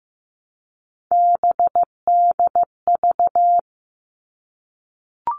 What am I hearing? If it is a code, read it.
Morse code, heard as a keyed tone, about 15 wpm, 700 Hz. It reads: BDV
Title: All Cut Numbers - 3-Digit 15wpm